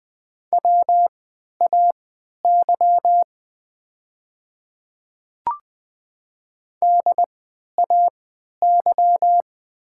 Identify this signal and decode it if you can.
Morse code, heard as a keyed tone, about 20 wpm, spaced as for 11 wpm, 700 Hz. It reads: WAY DAY
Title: 3x Character Spacing - Top 100 Words - 20wpm